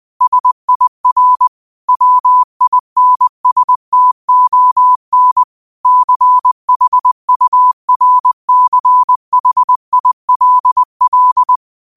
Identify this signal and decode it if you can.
Morse code, heard as a keyed tone, about 20 wpm, 1000 Hz. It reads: SIR WINSTON CHURCHILL
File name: qotdfiles / 20wpm